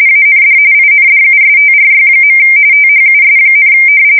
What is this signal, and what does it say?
RTTY, about 45.45 baud. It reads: AND INDUSTRY. THUS,
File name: fsk170 / battleshipnj